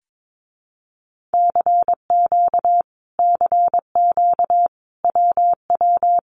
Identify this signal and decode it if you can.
Morse code, heard as a keyed tone, about 22 wpm, 700 Hz. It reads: CQ CQ WW
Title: Common QSO Phrases Encore 22wpm